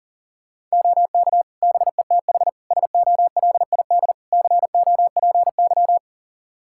Morse code, heard as a keyed tone, about 40 wpm, 700 Hz. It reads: OK BETH SOLID COPY
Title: Practice QSO - Exchange 3 - More Info - 40wpm